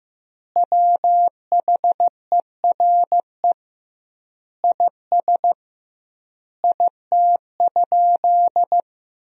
Morse code, heard as a keyed tone, about 15 wpm, 700 Hz. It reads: WHERE IS IT?
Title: 2x Word-Spacing - Sentences from Top 200 Words - 15wpm